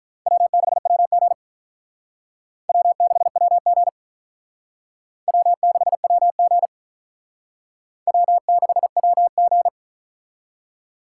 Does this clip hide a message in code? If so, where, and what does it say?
Yes: Morse code from 0.3 to 9.7 s, heard as a keyed tone, about 45 wpm, 700 Hz. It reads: W6WG W6WG W6WG W6WG